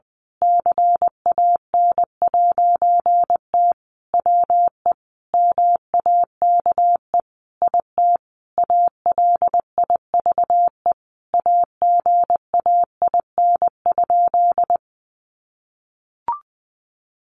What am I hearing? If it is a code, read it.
Morse code, heard as a keyed tone, about 20 wpm, 700 Hz. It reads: CAN'T WE MAKE IT ALIVE AGAIN?
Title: A Sound Of Thunder 20wpm